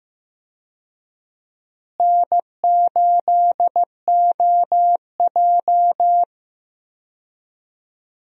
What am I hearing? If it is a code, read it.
Morse code, heard as a keyed tone, about 15 wpm, 700 Hz. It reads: N8OJ
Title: USA Call Signs - 4 Characters - Repeated 2x - 15wpm